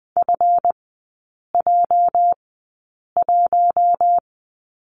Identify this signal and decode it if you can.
Morse code, heard as a keyed tone, about 20 wpm, spaced as for 8 wpm, 700 Hz. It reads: FJ1
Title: 218 Lesson 35e - Number 0 - Three Characters 20wpm